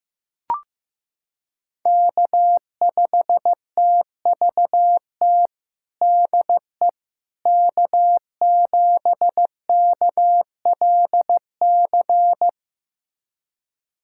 Morse code, heard as a keyed tone, about 15 wpm, 700 Hz. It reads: K5TVT DE K7KLC
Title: USA Callsign DE Callsign - 5 Characters - 15wpm